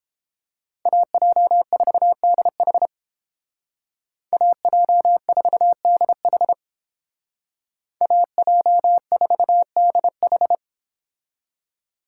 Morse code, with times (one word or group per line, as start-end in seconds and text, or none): 0.9-2.9 AJ4DH
4.3-6.5 AJ4DH
8.0-10.6 AJ4DH